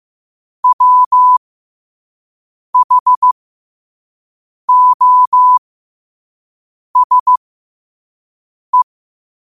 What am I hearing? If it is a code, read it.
Morse code, heard as a keyed tone, about 15 wpm, spaced as for 5 wpm, 1000 Hz. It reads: WHOSE
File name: qotdfiles / 5wpm